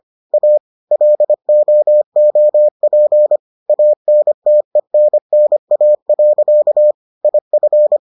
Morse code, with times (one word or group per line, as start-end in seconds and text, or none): 0.3-0.6 A
0.9-3.4 LOOP
3.7-6.9 ANTENNA.
7.2-8.0 IF